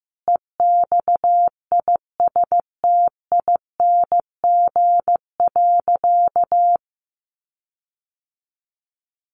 Morse code, heard as a keyed tone, about 15 wpm, 700 Hz. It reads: EXISTING.
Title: Sentences from Top 1500 Words 15wpm